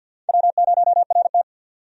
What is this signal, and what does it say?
Morse code, heard as a keyed tone, about 48 wpm, 700 Hz. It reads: W0RT
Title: USA Call Signs - 4 Characters - 50wpm